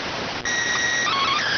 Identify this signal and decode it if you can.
SSTV (slow-scan television), PD 290.